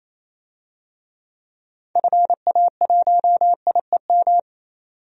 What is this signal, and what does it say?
Morse code, heard as a keyed tone, about 28 wpm, 700 Hz. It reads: FA1IEM
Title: DXCC-100 Call Signs - 6 Characters - 28wpm